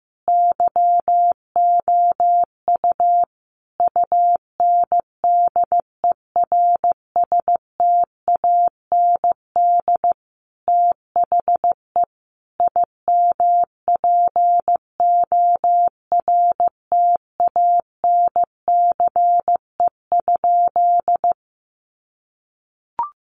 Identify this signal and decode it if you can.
Morse code, heard as a keyed tone, about 15 wpm, 700 Hz. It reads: YOU UNDERSTAND THE IMPORTANCE?
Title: Sentences from Top 1200 Words 15wpm